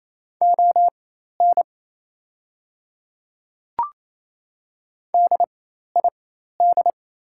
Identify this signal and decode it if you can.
Morse code, heard as a keyed tone, about 28 wpm, spaced as for 13 wpm, 700 Hz. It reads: ON DID
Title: 4x Character Spacing - Top 100 Words - 28wpm